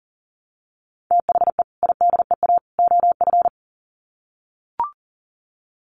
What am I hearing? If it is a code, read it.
Morse code, heard as a keyed tone, about 40 wpm, 700 Hz. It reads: THE IDEA OF
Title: Sets of 3 Words 40wpm